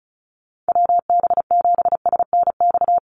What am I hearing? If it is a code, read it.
Morse code, heard as a keyed tone, about 34 wpm, 700 Hz. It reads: WB7SNX
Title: USA Call Signs Repeated 3x 35wpm